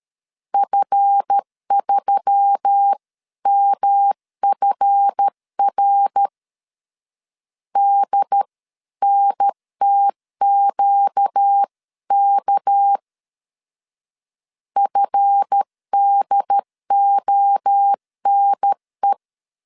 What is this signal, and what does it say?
Morse code, heard as a keyed tone, about 13 wpm, 800 Hz. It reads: F3MFR DNTQK FDONE
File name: LEZIONI / 7 GF9Z / 7 GF 12wpm 2122